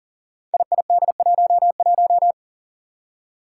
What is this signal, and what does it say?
Morse code, heard as a keyed tone, about 40 wpm, 700 Hz. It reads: IID11
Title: Five Challenging Letter-Number Combinations 40wpm